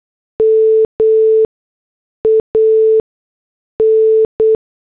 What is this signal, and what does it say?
Morse code, heard as a keyed tone, about 8 wpm, 440 Hz. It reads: M A N
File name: morse / headlines